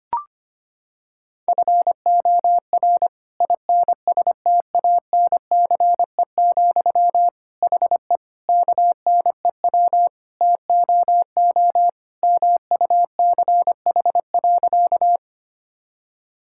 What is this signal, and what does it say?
Morse code, heard as a keyed tone, about 25 wpm, 700 Hz. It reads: FOR INSTANCE, HE KNEW TOO MUCH.
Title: Sentences from Top 1000 Words - Review Part 5 - 25wpm